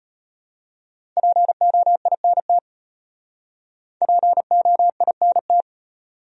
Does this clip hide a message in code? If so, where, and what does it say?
Yes: Morse code from 1.2 to 5.6 s, heard as a keyed tone, about 34 wpm, 700 Hz. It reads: POINT POINT